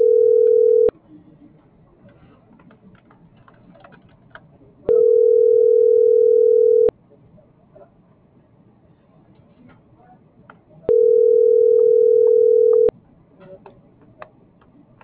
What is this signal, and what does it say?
Ringback tone (North America), 440 + 480 Hz.